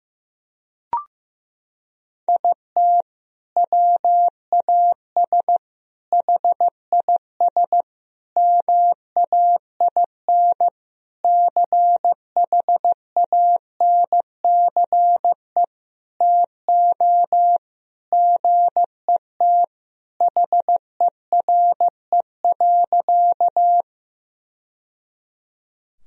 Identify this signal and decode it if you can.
Morse code, heard as a keyed tone, about 15 wpm, 700 Hz. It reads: IT WAS HIS MAIN CHANCE TO GET HERE.
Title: Sentences from Top 1000 Words - Review Part 2 - 15wpm